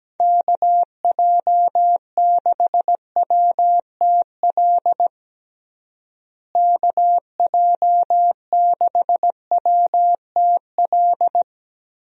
Morse code, heard as a keyed tone, about 17 wpm, 700 Hz. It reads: KJ6WTL KJ6WTL
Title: USA Call Signs - 6 Characters - Repeated 3x - 17wpm